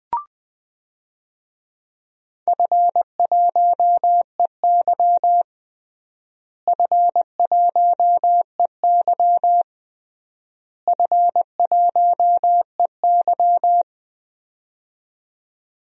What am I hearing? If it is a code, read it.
Morse code, heard as a keyed tone, about 21 wpm, 700 Hz. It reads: F1EY F1EY F1EY
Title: DXCC-100 Call Signs - 4 Characters - Repeated 3x - 20wpm